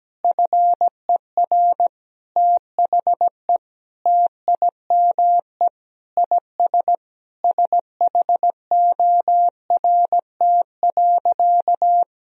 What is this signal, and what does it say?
Morse code, heard as a keyed tone, about 17 wpm, 700 Hz. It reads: FER THE TIME IS SHORT.
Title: Sentences from Top 500 Words and Abbreviations 17wpm